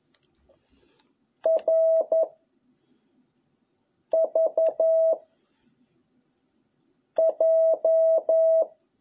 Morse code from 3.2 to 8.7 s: V J